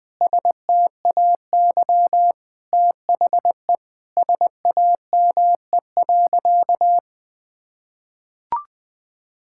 Morse code, heard as a keyed tone, about 20 wpm, 700 Hz. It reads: STAY THE SAME.
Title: Sentences with 7-Word Phrases - Spoken Beginning - 20wpm